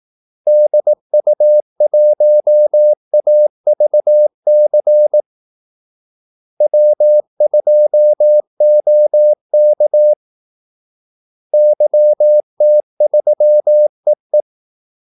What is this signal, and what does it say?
Morse code, heard as a keyed tone, about 18 wpm, 600 Hz. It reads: DU1AVC W2OK YT3EE
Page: Kaldesignaler 18 wpm